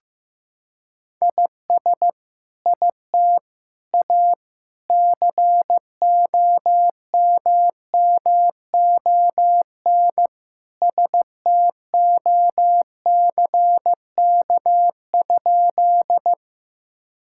Morse code, heard as a keyed tone, about 15 wpm, 700 Hz. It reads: IS IT A COMMON STOCK?
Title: Sentences from Top 1000 Words - Review Part 5 - 15wpm